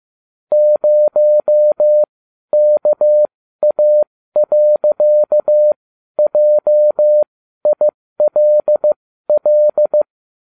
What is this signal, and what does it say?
Morse code, heard as a keyed tone, about 15 wpm, spaced as for 12 wpm, 600 Hz. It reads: 0KA.JILL